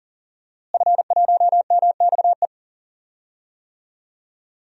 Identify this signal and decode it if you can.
Morse code, heard as a keyed tone, about 40 wpm, 700 Hz. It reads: F1MXE
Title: DXCC-100 Call Signs - 5 Characters - Repeated 3x - 40wpm